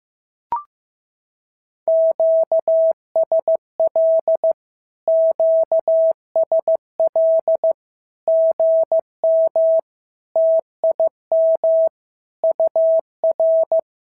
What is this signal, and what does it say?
Morse code, heard as a keyed tone, about 14 wpm, 650 Hz. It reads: QSL QSL GM TIM UR
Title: SOTA QSO Practice 15wpm